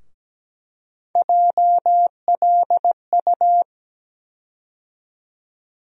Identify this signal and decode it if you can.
Morse code, heard as a keyed tone, about 17 wpm, 700 Hz. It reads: JLU
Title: USA Call Sign Postfixes - 3 Letters 17wpm